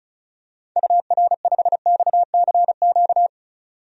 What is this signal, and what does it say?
Morse code, heard as a keyed tone, about 34 wpm, 700 Hz. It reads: UR5XCQ
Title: DXCC-100 Call Signs - 6 Characters - 35wpm